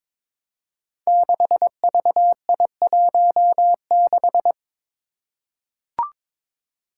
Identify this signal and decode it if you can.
Morse code, heard as a keyed tone, about 22 wpm, 700 Hz. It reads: 6VI16
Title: Five Challenging Letter-Number Combinations 22wpm